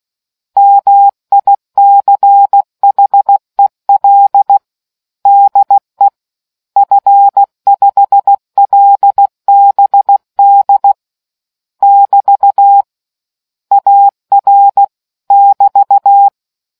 Morse code, heard as a keyed tone, about 16 wpm, 800 Hz. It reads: MICHEL DE F5LBD <BT> AR <BT>